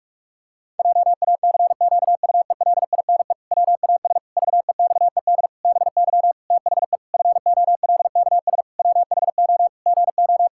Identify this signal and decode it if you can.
Morse code, heard as a keyed tone, about 46 wpm, 700 Hz. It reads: JACQUELINE WAS VEXED BY THE FOLKS WHO GO
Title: Pangram Challenge v2 - 45wpm